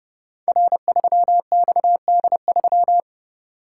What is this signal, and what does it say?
Morse code, heard as a keyed tone, about 30 wpm, 700 Hz. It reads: R3XD3
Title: Code Groups - 5 Letter:Numbers - 30wpm